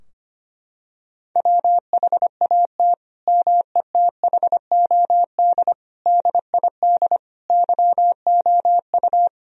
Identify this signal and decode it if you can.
Morse code, heard as a keyed tone, about 25 wpm, 700 Hz. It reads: WHAT METHOD DID YOU
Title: Sentences from Top 1000 Words - Review Part 3 - 25wpm